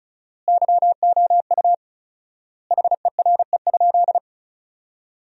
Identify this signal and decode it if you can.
Morse code, heard as a keyed tone, about 34 wpm, 700 Hz. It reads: YOU HERE?
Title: 4x Word-Spacing - Sentences from Top 500 Words - 35wpm